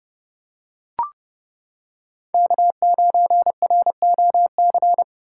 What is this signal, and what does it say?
Morse code, heard as a keyed tone, about 29 wpm, 700 Hz. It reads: K9ROC